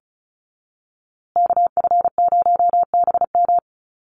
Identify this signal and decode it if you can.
Morse code, heard as a keyed tone, about 34 wpm, 700 Hz. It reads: KF0BM
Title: USA Call Signs Repeated 2x 35wpm